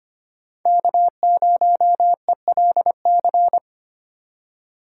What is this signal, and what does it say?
Morse code, heard as a keyed tone, about 33 wpm, 700 Hz. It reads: K0ELC